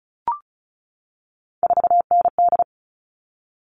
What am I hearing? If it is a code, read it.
Morse code, heard as a keyed tone, about 34 wpm, 700 Hz. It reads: 4ND
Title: All Cut Numbers - 3-Digit 35wpm